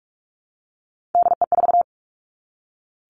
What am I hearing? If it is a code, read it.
Morse code, heard as a keyed tone, about 46 wpm, 700 Hz. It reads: DE4
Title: Three Challenging Letter-Number Combinations 45wpm